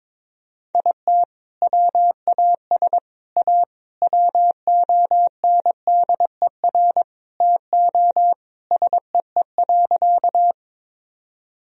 Morse code, heard as a keyed tone, about 22 wpm, 700 Hz. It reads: IT WAS A WONDER TO SEE.
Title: Sentences from Top 1000 Words - Review Part 3 - 22wpm